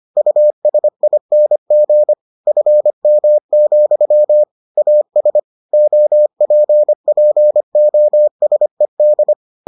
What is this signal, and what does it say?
Morse code, heard as a keyed tone, about 25 wpm, 600 Hz. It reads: USING FM, AS OPPOSED